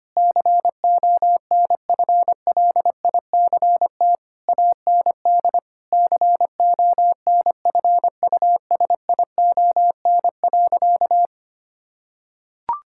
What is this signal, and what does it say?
Morse code, heard as a keyed tone, about 25 wpm, 700 Hz. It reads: CONFLICT AND CONFUSION.
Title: Sentences from Top 1600 Words 25wpm